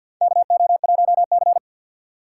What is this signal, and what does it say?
Morse code, heard as a keyed tone, about 53 wpm, 700 Hz. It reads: KO1C
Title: USA Call Signs - 4 Characters - 3x Character Spacing - 50wpm